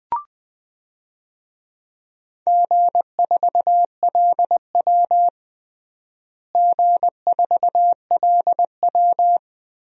Morse code, heard as a keyed tone, about 20 wpm, 700 Hz. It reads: G4LW G4LW
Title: DXCC-100 Call Signs - 4 Characters - Repeated 2x - 20wpm